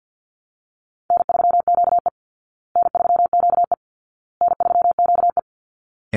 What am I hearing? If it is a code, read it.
Morse code, heard as a keyed tone, about 50 wpm, 700 Hz. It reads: N3QE N3QE N3QE
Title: Easy Call Signs Repeated 3x - 50wpm